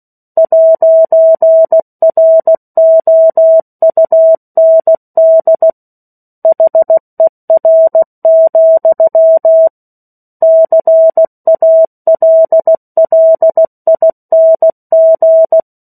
Morse code, heard as a keyed tone, about 16 wpm, 650 Hz. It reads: 'ROUND HER, CALLING
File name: 03-Chapter-03 16 WPM